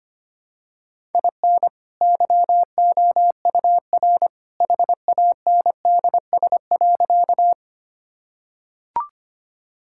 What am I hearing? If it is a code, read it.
Morse code, heard as a keyed tone, about 25 wpm, 700 Hz. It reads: IN YOUR HANDS.
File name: practice / 25wpm / Sentences with 2-Word Phrases - 25wpm